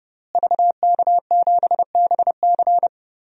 Morse code, heard as a keyed tone, about 30 wpm, 700 Hz. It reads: VK7BC
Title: DXCC-100 Call Signs - 5 Characters - Repeated 3x - 30wpm